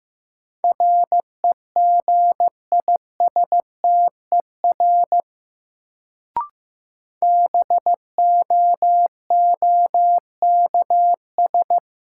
Morse code, heard as a keyed tone, about 15 wpm, 700 Hz. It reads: REGISTER BOOKS
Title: Sentences from Top 1200 Words - New Word Drill Alt - 15wpm